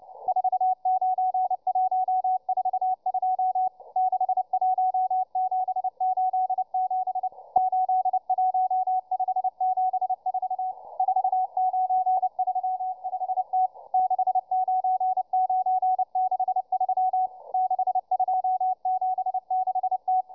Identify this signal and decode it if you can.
Morse code, heard as a keyed tone, about 29 wpm, 740 Hz, with radio noise. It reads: 49142 61787 81574 4935T 69963 6276T